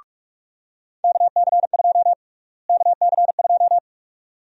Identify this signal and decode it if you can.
Morse code, heard as a keyed tone, about 41 wpm, 700 Hz. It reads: KC2 KC2
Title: USA Call Sign Prefixes - Repeated 2x - 45wpm